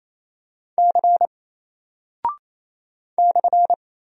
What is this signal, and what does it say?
Morse code, heard as a keyed tone, about 27 wpm, 700 Hz. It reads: C /
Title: Single Character (Letter:Number:Punctuation:Prosign) 28wpm